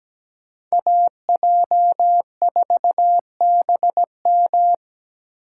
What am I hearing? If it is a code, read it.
Morse code, heard as a keyed tone, about 17 wpm, 700 Hz. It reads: AJ4BM
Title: USA Call Signs - 8x Character Spacing - 17wpm